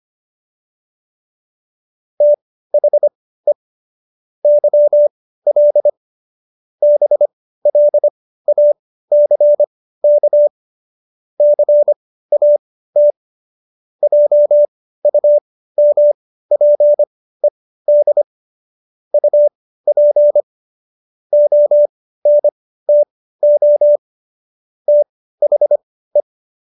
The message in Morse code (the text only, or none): THE YL BLACK CAT JUMPED UP ONTO THE